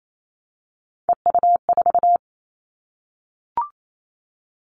Morse code, heard as a keyed tone, about 28 wpm, 700 Hz. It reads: EU4
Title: Callsign Prefixes for Top 50 DXCC - 28wpm